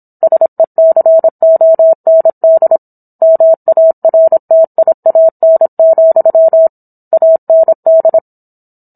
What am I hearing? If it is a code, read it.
Morse code, heard as a keyed tone, about 27 wpm, 650 Hz. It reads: SECOND MARTIAN, AND